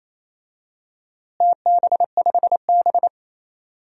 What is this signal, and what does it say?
Morse code, heard as a keyed tone, about 28 wpm, 700 Hz. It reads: TB5B
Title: DXCC-100 Call Signs - 4 Characters - Repeated 3x - 28wpm